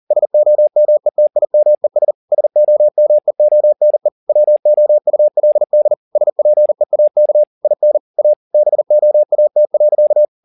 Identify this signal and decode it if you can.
Morse code, heard as a keyed tone, about 40 wpm, 600 Hz. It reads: SOMETIMES SOMEONE WOULD SPEAK IN A BOAT.